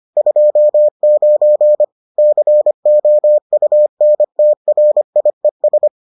Morse code, heard as a keyed tone, about 25 wpm, 600 Hz. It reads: 29 COUNTRIES